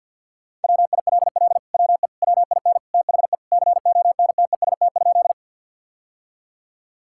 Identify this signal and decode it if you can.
Morse code, heard as a keyed tone, about 50 wpm, 700 Hz. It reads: WILL WE WIN THE CONTEST?
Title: Sentences from Top 1600 Words 50wpm